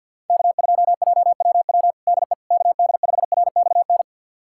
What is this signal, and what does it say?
Morse code, heard as a keyed tone, about 47 wpm, 700 Hz. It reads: K2JWW DE KD5RXN
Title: USA Callsign DE Callsign 50wpm